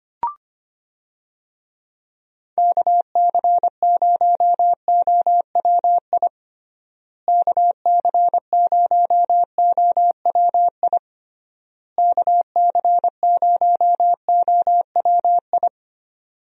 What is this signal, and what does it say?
Morse code, heard as a keyed tone, about 25 wpm, 700 Hz. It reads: KC0OWI KC0OWI KC0OWI
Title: USA Call Signs - 6 Characters - Repeated 3x - 25wpm